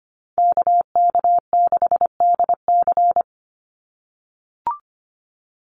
Morse code, heard as a keyed tone, about 25 wpm, 690 Hz. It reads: KK6DC
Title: USA Call Signs 25wpm